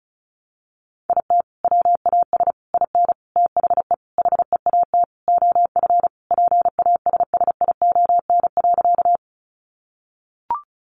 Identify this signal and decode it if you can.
Morse code, heard as a keyed tone, about 34 wpm, 700 Hz. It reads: IT WAS IN THE HEAT OF PASSION.
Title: Sentences from Top 1300 Words 35wpm